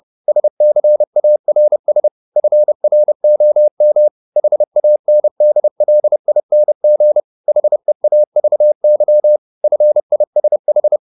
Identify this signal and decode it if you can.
Morse code, heard as a keyed tone, about 30 wpm, 600 Hz. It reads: SCARS FROM HANDLING HEAVY FISH
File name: the old man and the sea / the old man and the sea 1 30wpm